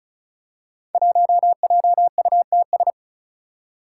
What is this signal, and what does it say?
Morse code, heard as a keyed tone, about 36 wpm, 700 Hz. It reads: 1JUTS
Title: Five Challenging Letter-Number Combinations 35wpm